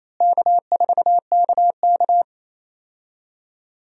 Morse code, heard as a keyed tone, about 27 wpm, 700 Hz. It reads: K4KK
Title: USA Call Signs - 4 Characters - Repeated 3x - 28wpm